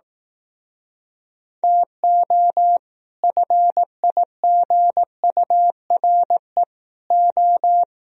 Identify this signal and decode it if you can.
Morse code, heard as a keyed tone, about 18 wpm, 700 Hz. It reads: TO FIGURE O
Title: Speed-Racing - Sets of 5 Words 15wpm